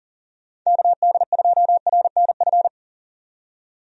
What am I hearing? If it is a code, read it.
Morse code, heard as a keyed tone, about 40 wpm, 700 Hz. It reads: KD2RNF
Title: USA Call Signs - 6 Characters - 40wpm